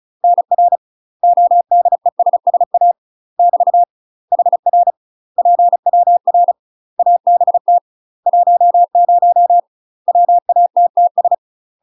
Morse code, heard as a keyed tone, about 34 wpm, 700 Hz. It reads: NR ODESSA <BT> HR PWR ABT 10 WATTS
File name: UK MASTER QSOs TRAINING / 35WPM / UK MASTER QSOs 0034